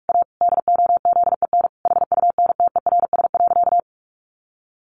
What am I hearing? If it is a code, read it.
Morse code, heard as a keyed tone, about 44 wpm, 700 Hz. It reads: A DOZEN HUNTERS.
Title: A Sound Of Thunder 45wpm